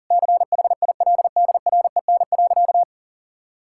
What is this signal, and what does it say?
Morse code, heard as a keyed tone, about 37 wpm, 700 Hz. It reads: CHILDREN.
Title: Sentences from Top 1000 Words - Review Part 5 - 40wpm